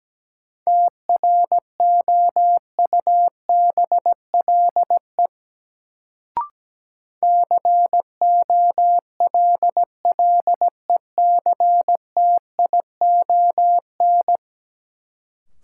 Morse code, heard as a keyed tone, about 17 wpm, 700 Hz. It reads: TROUBLE COLLECTION
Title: Sentences from Top 1200 Words - New Word Drill - 17wpm